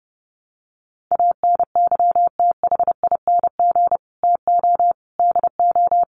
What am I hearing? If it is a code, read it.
Morse code, heard as a keyed tone, about 28 wpm, 700 Hz. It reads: ANYTHING TO DO
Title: Sets of 4 Words 30wpm